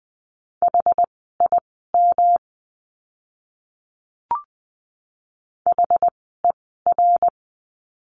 Morse code, heard as a keyed tone, about 20 wpm, spaced as for 14 wpm, 700 Hz. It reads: HIM HER
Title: 237 Lesson 38f - Letter q - Words 20wpm